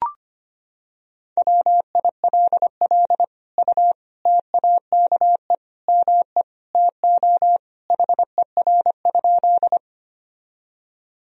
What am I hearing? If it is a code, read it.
Morse code, heard as a keyed tone, about 25 wpm, 700 Hz. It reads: WILL U TAKE ME TO HER?
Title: Sentences from Top 200 Words and Abbreviations 25wpm